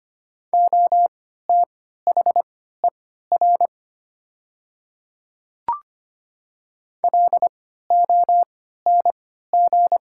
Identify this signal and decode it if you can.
Morse code, heard as a keyed tone, about 25 wpm, spaced as for 14 wpm, 700 Hz. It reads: OTHER LONG
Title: 3x Character Spacing - Top 500 Words - 25wpm